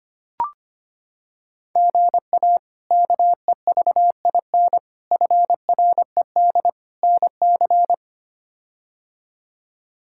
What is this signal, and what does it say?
Morse code, heard as a keyed tone, about 25 wpm, 700 Hz. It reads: GA KEVIN FRED NC
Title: K1USN SST QSO Practice - 25wpm